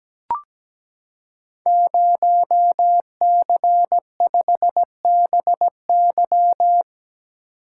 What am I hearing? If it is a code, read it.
Morse code, heard as a keyed tone, about 17 wpm, 700 Hz. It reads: 0C5BY
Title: Code Groups - 5 Letter:Numbers - 17wpm